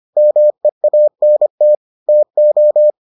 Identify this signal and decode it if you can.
Morse code, heard as a keyed tone, about 25 wpm, 600 Hz. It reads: MEANT TO